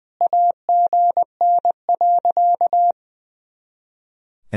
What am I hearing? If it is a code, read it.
Morse code, heard as a keyed tone, about 20 wpm, 700 Hz. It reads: AGN.
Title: Sentences from Top 500 Words and Abbreviations 20wpm